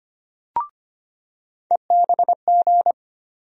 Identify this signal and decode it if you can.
Morse code, heard as a keyed tone, about 25 wpm, 700 Hz. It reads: EBG
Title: USA Call Sign Postfixes - 3 Letters 25wpm